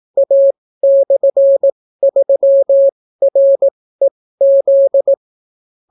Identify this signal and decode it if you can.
Morse code, heard as a keyed tone, about 18 wpm, spaced as for 15 wpm, 550 Hz. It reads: A/3REZ